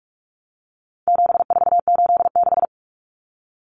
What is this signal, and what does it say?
Morse code, heard as a keyed tone, about 44 wpm, 700 Hz. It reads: 7486
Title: Common Cut Numbers - 4-Digit 45wpm